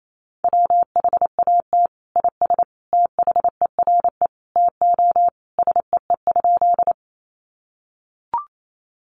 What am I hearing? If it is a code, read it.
Morse code, heard as a keyed tone, about 28 wpm, 700 Hz. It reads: WHAT IS THERE TO SEE?
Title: Sentences from Top 100 Words 28wpm